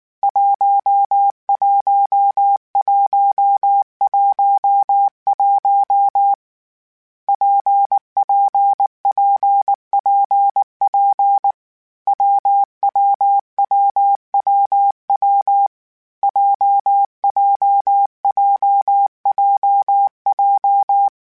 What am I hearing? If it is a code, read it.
Morse code, heard as a keyed tone, about 19 wpm, 800 Hz. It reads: 11111 PPPPP WWWWW JJJJJ
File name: Variable Speed / Variable Speed-mp3s / WPJ WPJ1 sp @18